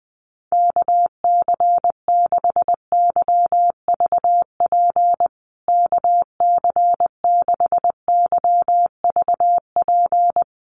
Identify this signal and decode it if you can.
Morse code, heard as a keyed tone, about 20 wpm, 700 Hz. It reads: KC6YVP KC6YVP
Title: USA Call Signs Repeated 3x 20wpm